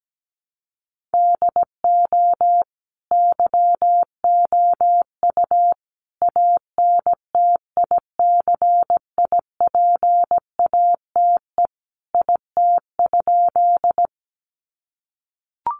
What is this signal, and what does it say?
Morse code, heard as a keyed tone, about 17 wpm, 700 Hz. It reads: DO YOU ANTICIPATE IT?
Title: Sentences from Top 1500 Words 17wpm